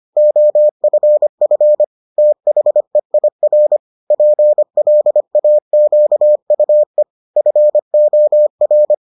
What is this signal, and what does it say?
Morse code, heard as a keyed tone, about 25 wpm, 600 Hz. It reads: OFF THEIR PLAQUE FOR